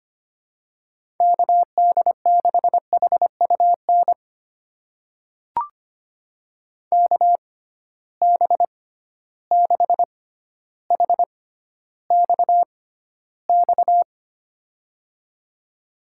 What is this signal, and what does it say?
Morse code, heard as a keyed tone, about 25 wpm, spaced as for 13 wpm, 700 Hz. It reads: KD6HUN KB6HXX
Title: USA Call Signs - 6 Characters - 6x Character Spacing - 25wpm